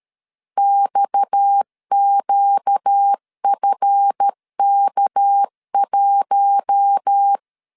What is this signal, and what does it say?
Morse code, heard as a keyed tone, about 13 wpm, 800 Hz. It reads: XQFK1
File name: LEZIONI / 8 YJ45 / 8 Y 12wpm 2228